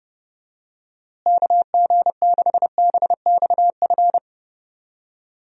Morse code, heard as a keyed tone, about 30 wpm, 700 Hz. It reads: KG6BXF
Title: USA Call Signs - 6 Characters - Repeated 3x - 30wpm